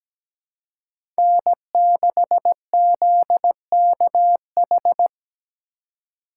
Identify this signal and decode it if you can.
Morse code, heard as a keyed tone, about 17 wpm, 700 Hz. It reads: N6ZKH
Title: USA Call Signs - 5 Characters - 4x Character Spacing - 17wpm